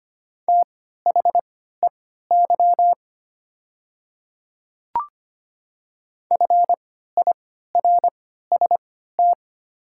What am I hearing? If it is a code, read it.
Morse code, heard as a keyed tone, about 25 wpm, spaced as for 14 wpm, 700 Hz. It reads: THEY FIRST
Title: 3x Character Spacing - Top 500 Words - 25wpm